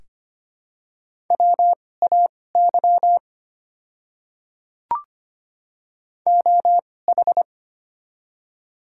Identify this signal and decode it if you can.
Morse code, heard as a keyed tone, about 25 wpm, spaced as for 18 wpm, 700 Hz. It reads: WAY OH
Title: 2x Character Spacing - Top 500 Words - 25wpm